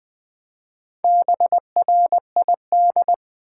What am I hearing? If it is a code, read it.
Morse code, heard as a keyed tone, about 21 wpm, 700 Hz. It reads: BRID
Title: Build A Word - 2 Letters At A Time - 6-Letter Words - 20wpm